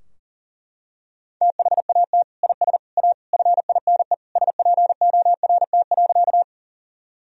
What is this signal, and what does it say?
Morse code, heard as a keyed tone, about 40 wpm, 700 Hz. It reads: THAT IS A FINE SPORT.
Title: Sentences from Top 1000 Words - Review Part 4 - 40wpm